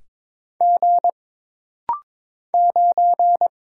Morse code, heard as a keyed tone, about 22 wpm, 700 Hz. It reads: G 9
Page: This rapid-fire timing is geared to push your ICR (Instant Character Recognition) skills.